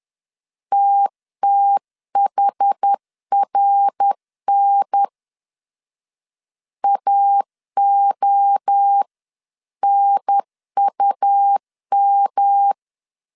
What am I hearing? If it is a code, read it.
Morse code, heard as a keyed tone, about 10 wpm, 800 Hz. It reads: TTHRN AONUM
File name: 10wpm